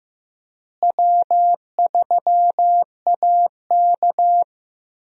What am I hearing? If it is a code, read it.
Morse code, heard as a keyed tone, about 16 wpm, 700 Hz. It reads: W3AK
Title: USA Call Signs - 4 Characters - Repeated 2x - 15wpm